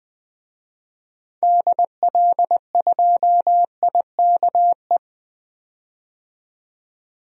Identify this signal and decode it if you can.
Morse code, heard as a keyed tone, about 20 wpm, 700 Hz. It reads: DL2IKE
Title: DXCC-100 Call Signs - 6 Characters - 20wpm